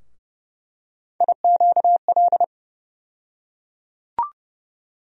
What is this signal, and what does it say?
Morse code, heard as a keyed tone, about 30 wpm, 700 Hz. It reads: IQL
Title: USA Call Sign Postfixes - 3 Letters 30wpm